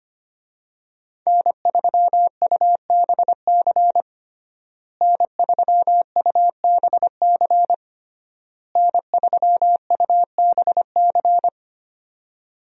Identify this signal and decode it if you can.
Morse code, heard as a keyed tone, about 25 wpm, 700 Hz. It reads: N3UBC N3UBC N3UBC
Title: USA Call Signs - 5 Characters - Repeated 3x - 25wpm